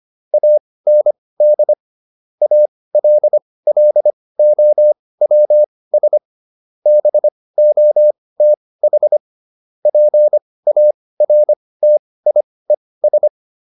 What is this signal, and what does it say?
Morse code, heard as a keyed tone, about 25 wpm, spaced as for 18 wpm, 600 Hz. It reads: AND ALLOWS BOTH PARTIES